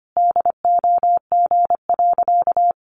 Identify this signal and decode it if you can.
Morse code, heard as a keyed tone, about 25 wpm, 700 Hz. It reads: DOG.
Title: Easy Phrases 25wpm